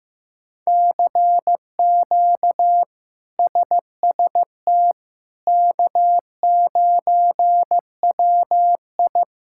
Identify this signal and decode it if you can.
Morse code, heard as a keyed tone, about 15 wpm, 700 Hz. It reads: CQ SST K9WI
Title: K1USN SST QSO Practice - 15wpm